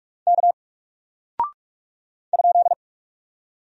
Morse code, heard as a keyed tone, about 44 wpm, 700 Hz. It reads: K ?
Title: Single Character (Letter:Number:Punctuation:Prosign) 45wpm